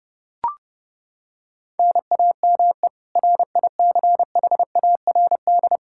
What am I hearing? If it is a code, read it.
Morse code, heard as a keyed tone, about 30 wpm, 700 Hz. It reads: NAME RICHARD
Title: Practice QSO - Exchange 1 - Beginning - 30wpm